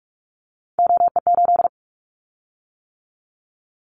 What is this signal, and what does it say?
Morse code, heard as a keyed tone, about 44 wpm, 700 Hz. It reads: OE8
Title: Callsign Prefixes for Top 100 DXCC - 45wpm